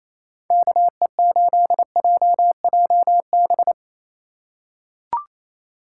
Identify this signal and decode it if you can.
Morse code, heard as a keyed tone, about 28 wpm, 700 Hz. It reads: KE8JJB
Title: USA Call Signs - 6 Characters - 4x Character Spacing - 28wpm